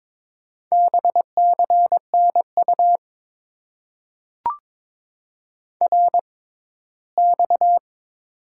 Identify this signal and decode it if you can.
Morse code, heard as a keyed tone, about 22 wpm, 700 Hz. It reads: B C N U R X
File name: Top 90 QSO Elements - 6x Character Spacing - 22wpm